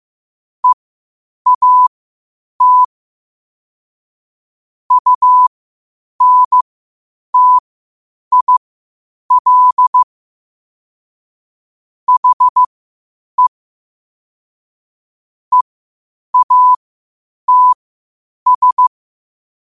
Morse code, heard as a keyed tone, about 15 wpm, spaced as for 8 wpm, 1000 Hz. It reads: EAT UNTIL HE EATS
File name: qotdfiles / 8wpm